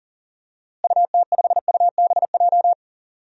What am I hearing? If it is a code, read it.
Morse code, heard as a keyed tone, about 40 wpm, 700 Hz. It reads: UT5UBJ
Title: DXCC-100 Call Signs - 6 Characters - Repeated 3x - 40wpm